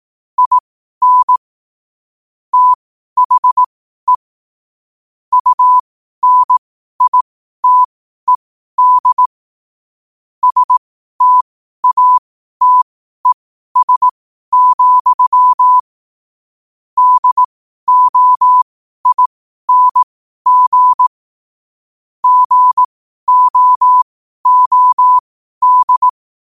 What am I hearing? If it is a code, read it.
Morse code, heard as a keyed tone, about 18 wpm, spaced as for 12 wpm, 1000 Hz. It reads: IN THE UNITED STATES, DOING GOOD
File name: qotdfiles / 12wpm